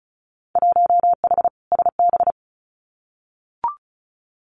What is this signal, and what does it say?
Morse code, heard as a keyed tone, about 34 wpm, 700 Hz. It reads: 1H SB
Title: Winter Field Day Exchanges 35wpm